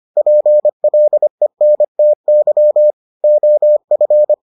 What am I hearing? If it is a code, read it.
Morse code, heard as a keyed tone, about 25 wpm, 600 Hz. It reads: PLENTY OF